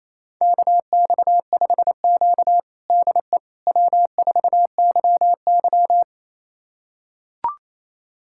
Morse code, heard as a keyed tone, about 28 wpm, 700 Hz. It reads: KX5Q DE W4YY
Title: USA Callsign DE Callsign - 4 Characters - 28wpm